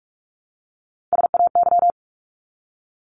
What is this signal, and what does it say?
Morse code, heard as a keyed tone, about 44 wpm, 700 Hz. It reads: SAY
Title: Top 200 Words 45wpm